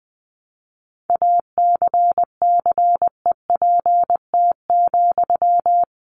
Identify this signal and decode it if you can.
Morse code, heard as a keyed tone, about 19 wpm, 700 Hz. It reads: ACCEPT,
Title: Sentences from Top 1300 Words 20wpm